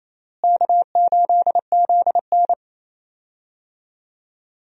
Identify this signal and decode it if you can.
Morse code, heard as a keyed tone, about 28 wpm, 700 Hz. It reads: K8ZN
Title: USA Call Signs - 4 Characters - Repeated 3x - 28wpm